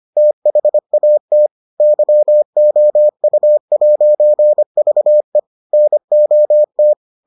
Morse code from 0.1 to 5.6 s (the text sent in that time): THAT YOU'VE